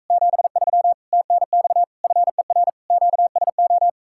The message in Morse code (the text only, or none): 73 TNX FER QSO